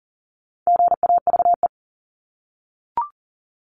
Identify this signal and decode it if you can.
Morse code, heard as a keyed tone, about 40 wpm, 700 Hz. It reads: GAVE
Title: Top 500 Words Repeated 40wpm